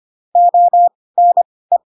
Morse code, heard as a keyed tone, about 25 wpm, spaced as for 18 wpm, 700 Hz. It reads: ONE
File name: Pinocchio / Pinocchio ch4 12 25 ik1hge